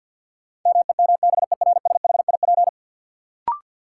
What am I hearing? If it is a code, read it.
Morse code, heard as a keyed tone, about 49 wpm, 700 Hz. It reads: MEMBERSHIP
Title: Sentences from Top 1400 Words - New Word Drill Alt - 50wpm